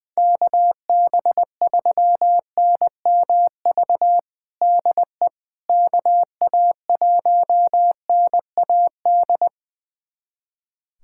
Morse code, heard as a keyed tone, about 20 wpm, 700 Hz. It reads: KB3NMV DE KA1NAD
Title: USA Callsign DE Callsign - 6 Characters - 20wpm